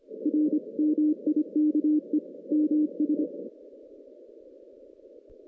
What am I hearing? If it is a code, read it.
Morse code, heard as a keyed tone, about 26 wpm, 300 Hz. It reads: R MIKE MS
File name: 2016 JAN CW NAQP / 2016 Jan naqp k5ot - 0-10